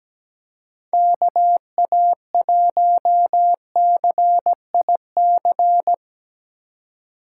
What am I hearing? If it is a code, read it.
Morse code, heard as a keyed tone, about 19 wpm, 700 Hz. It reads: KA1CIC
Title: Speed Racing - USA 2x3 Call Signs 17wpm